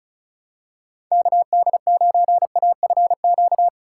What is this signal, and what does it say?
Morse code, heard as a keyed tone, about 34 wpm, 700 Hz. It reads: KD9AFQ
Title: USA Callsign DE Callsign - 6 Characters - 35wpm